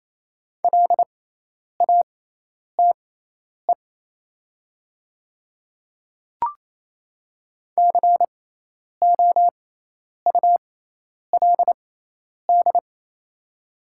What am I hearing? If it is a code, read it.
Morse code, heard as a keyed tone, about 27 wpm, spaced as for 10 wpm, 700 Hz. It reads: LATE COULD
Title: 6x Character Spacing - Top 300 Words - 28wpm